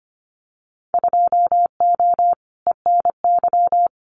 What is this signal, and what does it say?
Morse code, heard as a keyed tone, about 25 wpm, 700 Hz. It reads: 2O ENY
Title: Winter Field Day Exchanges 25wpm